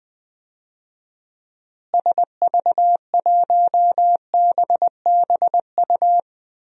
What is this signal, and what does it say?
Morse code, heard as a keyed tone, about 21 wpm, 700 Hz. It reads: SV1BBU
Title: DXCC-100 Call Signs - 6 Characters - Repeated 2x - 20wpm